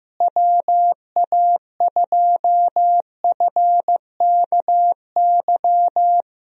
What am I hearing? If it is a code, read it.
Morse code, heard as a keyed tone, about 15 wpm, 700 Hz. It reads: WA2FKY
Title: USA Call Signs - 6 Characters - Repeated 3x - 15wpm